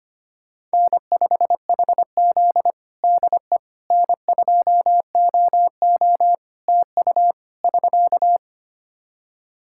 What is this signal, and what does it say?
Morse code, heard as a keyed tone, about 25 wpm, 700 Hz. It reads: N5HZ DE N2OO TU <SK>
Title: Practice QSO - Exchange 4 - Ending - 25wpm